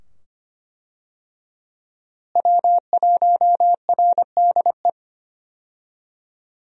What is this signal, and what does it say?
Morse code, heard as a keyed tone, about 25 wpm, 690 Hz. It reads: W1RDE
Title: USA Call Signs - 5 Characters - Repeated 2x - 25wpm